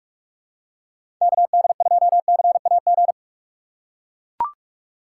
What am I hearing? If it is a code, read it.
Morse code, heard as a keyed tone, about 42 wpm, 700 Hz. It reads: KD2CAG
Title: USA Call Signs - 6x Character Spacing - 45wpm